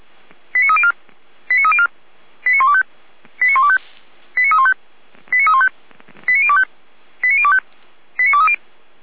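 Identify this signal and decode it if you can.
Five-tone selective call (ZVEI), heard as quick runs of five tones, tones of about 70 ms. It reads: 89304 89304 89125 89125 89315 89315 89025 89025 89230
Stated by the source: ZVEI five-tone selective call received on SDR.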